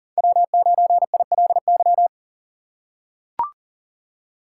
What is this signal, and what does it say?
Morse code, heard as a keyed tone, about 39 wpm, 700 Hz. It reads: W9ILY
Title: K1USN SST QSO Practice - 40wpm